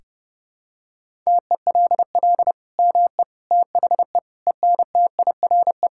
Morse code, heard as a keyed tone, about 30 wpm, 700 Hz. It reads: TELL ME THE ENTIRE
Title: Sentences from Top 1000 Words - Review Part 5 - 30wpm